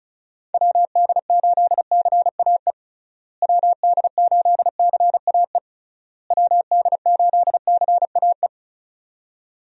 Morse code, heard as a keyed tone, about 36 wpm, 700 Hz. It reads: WD8CAE WD8CAE WD8CAE
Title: USA Call Signs - 6 Characters - Repeated 3x - 35wpm